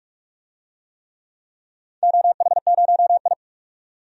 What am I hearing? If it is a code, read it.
Morse code, heard as a keyed tone, about 44 wpm, 700 Hz. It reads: OH0I
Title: DXCC-100 Call Signs - 4 Characters - 45wpm